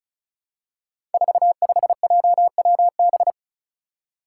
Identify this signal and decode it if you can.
Morse code, heard as a keyed tone, about 34 wpm, 700 Hz. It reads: 45JWB
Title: Five Challenging Letter-Number Combinations 35wpm